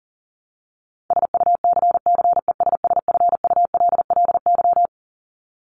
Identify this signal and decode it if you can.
Morse code, heard as a keyed tone, about 40 wpm, 700 Hz. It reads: SUCCESSFULLY
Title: LY Suffix - Top 100 Words with Base Words 40wpm